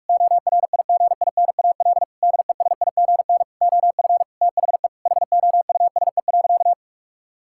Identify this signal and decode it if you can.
Morse code, heard as a keyed tone, about 46 wpm, 700 Hz. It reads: ORIGINAL DESIGN OF THE HOUSE.
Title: Sentences from Top 1000 Words - Review Part 6 - 45wpm